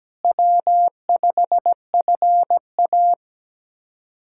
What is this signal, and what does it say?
Morse code, heard as a keyed tone, about 17 wpm, 700 Hz. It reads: W5FA
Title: USA Callsign DE Callsign - 4 Characters - 17wpm